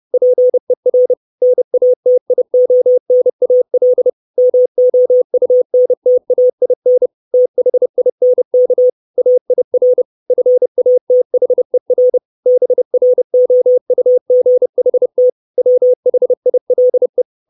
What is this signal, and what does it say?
Morse code, heard as a keyed tone, about 30 wpm, 500 Hz. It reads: PER NATIONAL MOUNTAIN THINK AIR FATHER BROUGHT WHILE
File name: 632 Words 30WPM